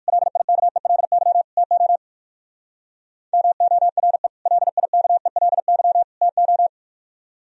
Morse code, heard as a keyed tone, about 58 wpm, 700 Hz. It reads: LIKELY TO MORE LIKELY TO